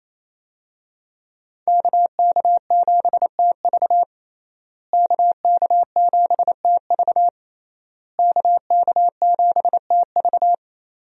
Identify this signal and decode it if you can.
Morse code, heard as a keyed tone, about 28 wpm, 700 Hz. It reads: KK7TV KK7TV KK7TV
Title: USA Call Signs - 5 Characters - Repeated 3x - 28wpm